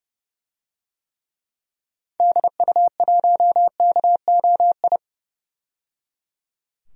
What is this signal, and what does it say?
Morse code, heard as a keyed tone, about 30 wpm, 700 Hz. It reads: DU1KOI
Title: DXCC-100 Call Signs - 6 Characters - 30wpm